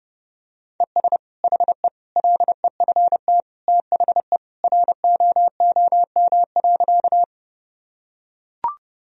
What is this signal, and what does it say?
Morse code, heard as a keyed tone, about 30 wpm, 700 Hz. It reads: ES HE LEFT THE ROOM.
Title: Sentences from Top 300 Words and Abbreviations 30wpm